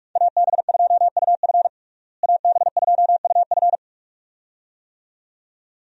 Morse code, heard as a keyed tone, about 45 wpm, 700 Hz. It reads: AB2UF AB2UF
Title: USA Call Signs - 5 Characters - Repeated 2x - 45wpm